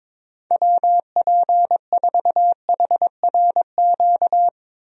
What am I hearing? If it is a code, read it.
Morse code, heard as a keyed tone, about 22 wpm, 700 Hz. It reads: WP4HRQ
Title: USA Call Signs - 6 Characters - 22wpm